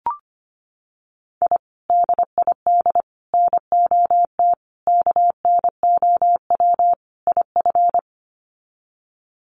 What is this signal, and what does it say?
Morse code, heard as a keyed tone, about 25 wpm, 700 Hz. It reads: I DID NOT KNOW IF
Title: Sets of 5 Words Encore 25wpm